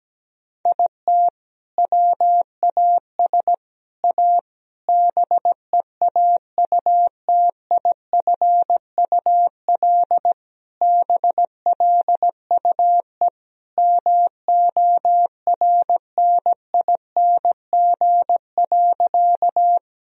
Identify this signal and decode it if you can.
Morse code, heard as a keyed tone, about 17 wpm, 700 Hz. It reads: IT WAS A BEAUTIFUL BLUE MORNING.
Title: Sentences from Top 1000 Words - Review Part 3 - 17wpm